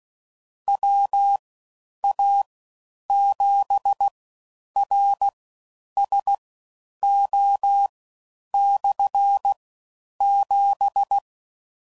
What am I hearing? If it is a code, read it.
Morse code, heard as a keyed tone, about 16 wpm, spaced as for 9 wpm, 800 Hz. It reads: WA7RSO/7
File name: Basic Sets / QST de WA7RSO / All Dits